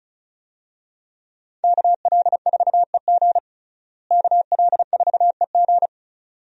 Morse code, heard as a keyed tone, about 36 wpm, 700 Hz. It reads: KL4EG KL4EG
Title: USA Call Signs - 5 Characters - Repeated 2x - 35wpm